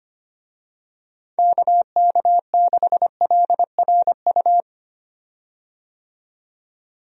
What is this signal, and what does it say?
Morse code, heard as a keyed tone, about 25 wpm, 700 Hz. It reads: KK6LRU
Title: USA Call Signs - 6 Characters - 25wpm